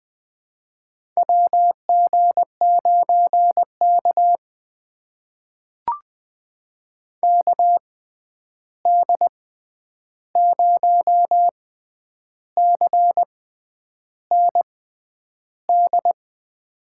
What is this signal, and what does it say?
Morse code, heard as a keyed tone, about 20 wpm, spaced as for 9 wpm, 700 Hz. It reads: WG9K KD0CND
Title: USA Call Signs - 6x Character Spacing - 20wpm